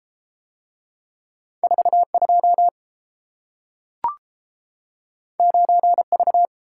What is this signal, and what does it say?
Morse code, heard as a keyed tone, about 31 wpm, 700 Hz. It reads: 42 9V